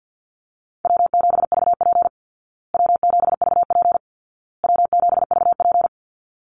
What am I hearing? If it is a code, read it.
Morse code, heard as a keyed tone, about 50 wpm, 700 Hz. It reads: W7VP W7VP W7VP
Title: Easy Call Signs Repeated 3x - 50wpm